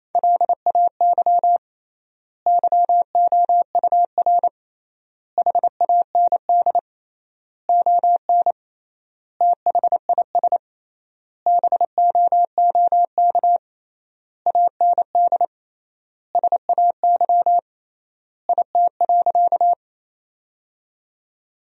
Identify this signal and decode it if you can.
Morse code, heard as a keyed tone, about 28 wpm, 700 Hz. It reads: LAY YOUR HAND ON THIS BOOK AND SAY IT.
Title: 3x Word-Spacing - Sentences from Top 400 Words - 28wpm